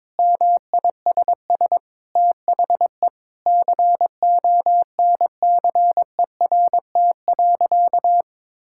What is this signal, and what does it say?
Morse code, heard as a keyed tone, about 22 wpm, 700 Hz. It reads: MISS THE CONCERT.
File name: practice / 22wpm / Sentences with 8-Word Phrases - 22wpm